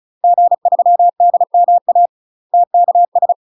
Morse code, heard as a keyed tone, about 35 wpm, 700 Hz. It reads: G3DMA TKS
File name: UK MASTER QSOs TRAINING / 35WPM / UK MASTER QSOs 0016